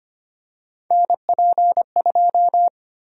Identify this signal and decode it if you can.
Morse code, heard as a keyed tone, about 25 wpm, 700 Hz. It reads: NP2
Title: USA Call Sign Prefixes - 25wpm